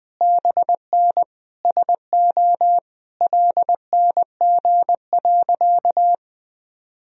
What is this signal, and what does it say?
Morse code, heard as a keyed tone, about 21 wpm, 700 Hz. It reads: BN SO LNG.
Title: Sentences from Top 100 Words and Abbreviations 20wpm